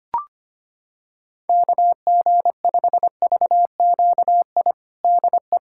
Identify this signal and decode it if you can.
Morse code, heard as a keyed tone, about 25 wpm, 700 Hz. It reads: KG5VQI DE
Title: USA Callsign DE Callsign 25wpm